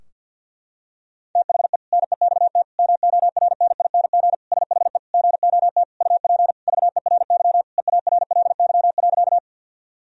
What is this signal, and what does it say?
Morse code, heard as a keyed tone, about 52 wpm, 700 Hz. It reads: THE NEXT MORNING SHE GOT UP VERY EARLY.
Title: Sentences from Top 1000 Words - Review Part 2 - 50wpm